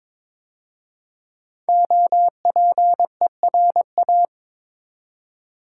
Morse code, heard as a keyed tone, about 23 wpm, 700 Hz. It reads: OPERA
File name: 5 Letter Words - First 3 Letters Given - 22wpm